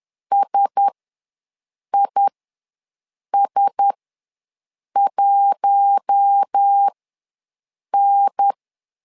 Morse code, heard as a keyed tone, about 10 wpm, spaced as for 6 wpm, 800 Hz. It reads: SIS1N
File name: LEZIONI / 4 Q1S ws 10wpm 2208